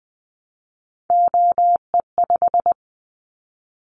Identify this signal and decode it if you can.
Morse code, heard as a keyed tone, about 20 wpm, 700 Hz. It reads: OE5
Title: Callsign Prefixes for Top 25 DXCC - Repeated 3x - 20wpm